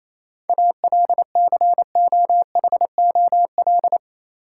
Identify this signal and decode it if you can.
Morse code, heard as a keyed tone, about 27 wpm, 700 Hz. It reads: ALCOHOL
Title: Sentences from Top 1400 Words - New Word Drill Alt - 28wpm